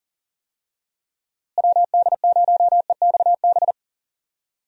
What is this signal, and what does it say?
Morse code, heard as a keyed tone, about 40 wpm, 700 Hz. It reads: WD0EXB
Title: USA Call Signs - 6 Characters - Repeated 2x - 40wpm